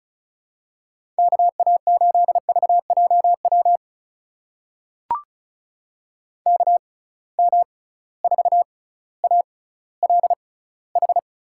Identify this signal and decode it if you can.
Morse code, heard as a keyed tone, about 36 wpm, spaced as for 18 wpm, 700 Hz. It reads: KA8VJW KM4ALH
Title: USA Call Signs - 6 Characters - 6x Character Spacing - 35wpm